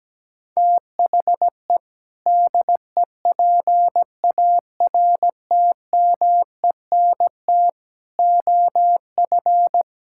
Morse code, heard as a keyed tone, about 17 wpm, 700 Hz. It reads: THE DEPARTMENT OF
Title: Sentences from Top 1000 Words - Review Part 5 - 17wpm